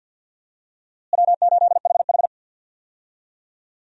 Morse code, heard as a keyed tone, about 52 wpm, 700 Hz. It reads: W8HH
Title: USA Call Signs - 4 Characters - Repeated 3x - 50wpm